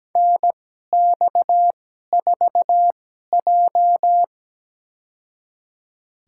Morse code, heard as a keyed tone, about 17 wpm, spaced as for 12 wpm, 700 Hz. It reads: NX4J
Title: USA Call Signs - 4 Characters - 2x Character Spacing - 17wpm